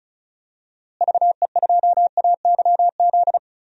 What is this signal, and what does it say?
Morse code, heard as a keyed tone, about 36 wpm, 700 Hz. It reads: VE2AYZ
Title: DXCC-100 Call Signs - 6 Characters - Repeated 3x - 35wpm